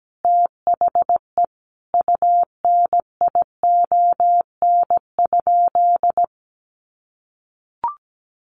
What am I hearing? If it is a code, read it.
Morse code, heard as a keyed tone, about 17 wpm, 700 Hz. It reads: THE UNION?
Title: Sentences from Top 1400 Words 17wpm